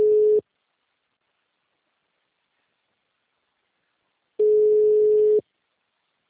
Ringback tone (Europe, 425 Hz).